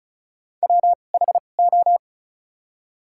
Morse code, heard as a keyed tone, about 36 wpm, spaced as for 25 wpm, 700 Hz. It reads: WHO
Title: 2x Character Spacing - Top 100 Words - 35wpm